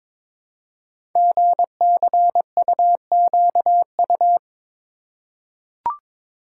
Morse code, heard as a keyed tone, about 22 wpm, 700 Hz. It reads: GCUQU
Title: Code Groups - 5 Letters - 22wpm